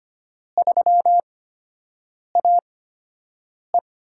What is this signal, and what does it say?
Morse code, heard as a keyed tone, about 25 wpm, spaced as for 7 wpm, 700 Hz. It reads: 3AE
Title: USA Call Signs - 8x Character Spacing - 25wpm